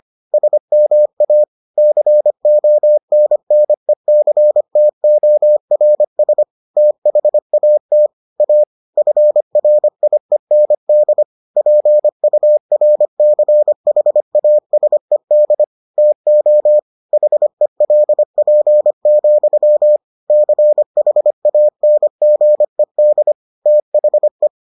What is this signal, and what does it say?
Morse code, heard as a keyed tone, about 25 wpm, 600 Hz. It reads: SMA CONNECTORS THAT A FRIEND PURCHASED TO HELP, CHANGED THE